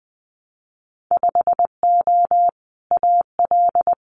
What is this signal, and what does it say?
Morse code, heard as a keyed tone, about 20 wpm, 700 Hz. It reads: 5O AL
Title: Winter Field Day Exchanges 20wpm